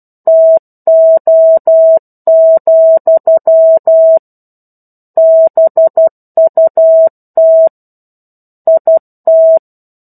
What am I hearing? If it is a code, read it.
Morse code, heard as a keyed tone, about 12 wpm, 650 Hz. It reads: TO, BUT IT
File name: 01-Chapter-01 12 WPM